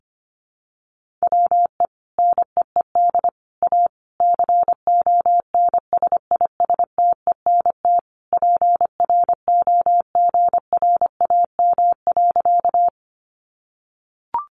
Morse code, heard as a keyed tone, about 25 wpm, 700 Hz. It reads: WE NEED A CONSISTENT PROGRAM.
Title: Sentences from Top 1500 Words 25wpm